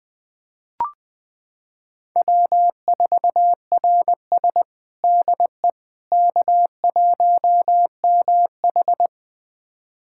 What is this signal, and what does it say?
Morse code, heard as a keyed tone, about 20 wpm, 700 Hz. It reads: W4RS DE K1MH
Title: USA Callsign DE Callsign - 4 Characters - 20wpm